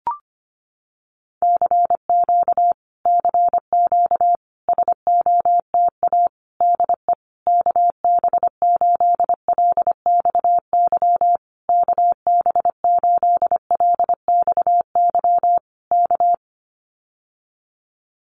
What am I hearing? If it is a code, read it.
Morse code, heard as a keyed tone, about 25 wpm, 700 Hz. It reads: CQ CQ SOTA DE KB8LXY KB8LXY K
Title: Common QSO Phrases Encore 25wpm